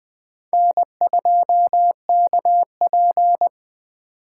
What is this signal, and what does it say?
Morse code, heard as a keyed tone, about 20 wpm, 700 Hz. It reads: N2KP